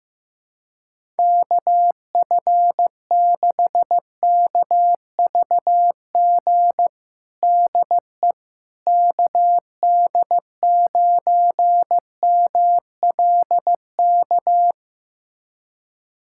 Morse code, heard as a keyed tone, about 15 wpm, 700 Hz. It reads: KF6KVG DE KD9MLK
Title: USA Callsign DE Callsign 15wpm